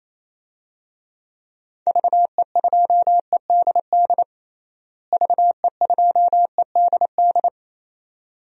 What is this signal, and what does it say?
Morse code, heard as a keyed tone, about 28 wpm, 700 Hz. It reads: VE2EDD VE2EDD
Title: DXCC-100 Call Signs - 6 Characters - Repeated 2x - 28wpm